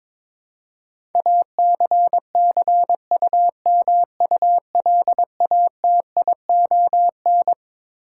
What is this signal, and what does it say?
Morse code, heard as a keyed tone, about 22 wpm, 700 Hz. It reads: ACCUMULATION
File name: Twelve Letter Words - 22wpm